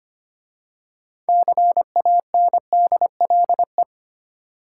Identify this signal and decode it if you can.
Morse code, heard as a keyed tone, about 25 wpm, 690 Hz. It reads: CANDLE
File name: practice / 25wpm / Easy Words with Hints - 25wpm